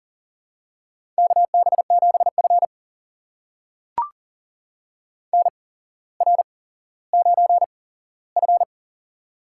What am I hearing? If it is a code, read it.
Morse code, heard as a keyed tone, about 40 wpm, spaced as for 17 wpm, 700 Hz. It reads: KB7F NR9F
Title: USA Call Signs - 4 Characters - 8x Character Spacing - 40wpm